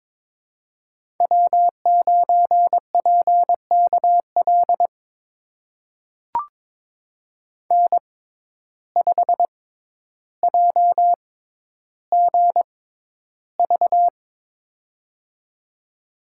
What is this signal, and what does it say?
Morse code, heard as a keyed tone, about 23 wpm, spaced as for 11 wpm, 700 Hz. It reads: W9PKL N5JGV
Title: USA Call Signs - 6x Character Spacing - 22wpm